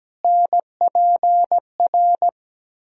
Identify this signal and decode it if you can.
Morse code, heard as a keyed tone, about 17 wpm, 700 Hz. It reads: NPR
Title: Speed-Racing - Sets of 5 Words 17wpm